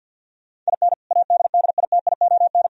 Morse code, heard as a keyed tone, about 63 wpm, 700 Hz. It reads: IN ADDITION